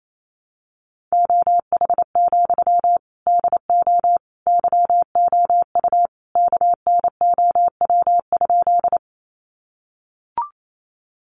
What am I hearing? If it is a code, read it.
Morse code, heard as a keyed tone, about 28 wpm, 700 Hz. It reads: OH, DO YOU KNOW?
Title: Sentences from Top 500 Words 28wpm